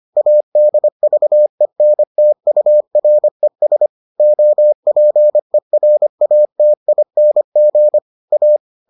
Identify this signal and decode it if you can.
Morse code, heard as a keyed tone, about 25 wpm, 600 Hz. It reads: ADVENTURES OPERATING A